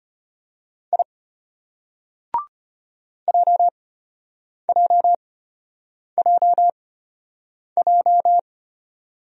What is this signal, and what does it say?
Morse code, heard as a keyed tone, about 36 wpm, spaced as for 8 wpm, 700 Hz. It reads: I JJJJ